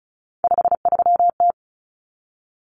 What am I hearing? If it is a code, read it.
Morse code, heard as a keyed tone, about 34 wpm, 700 Hz. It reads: 53T
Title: Common Cut Numbers - 3-Digit 35wpm